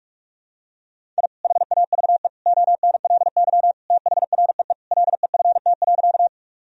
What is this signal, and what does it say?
Morse code, heard as a keyed tone, about 47 wpm, 700 Hz. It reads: I HAVE ONLY THREE LEFT.
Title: Sentences from Top 1000 Words - Review Part 1 - 45wpm